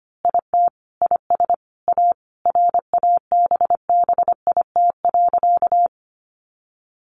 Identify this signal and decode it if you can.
Morse code, heard as a keyed tone, about 25 wpm, 700 Hz. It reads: IT IS A RABBIT.
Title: Easy Phrases 25wpm